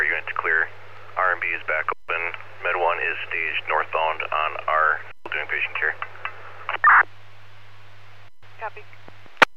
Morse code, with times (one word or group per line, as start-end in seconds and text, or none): none